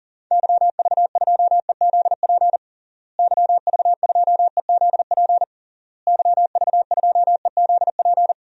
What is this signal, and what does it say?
Morse code, heard as a keyed tone, about 37 wpm, 700 Hz. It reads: YV2EZP YV2EZP YV2EZP
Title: DXCC-100 Call Signs - 6 Characters - Repeated 3x - 40wpm